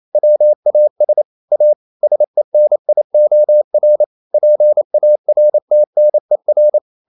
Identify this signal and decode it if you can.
Morse code, heard as a keyed tone, about 28 wpm, 600 Hz. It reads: WAS A SENIOR PARTNER
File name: Never 0027wpm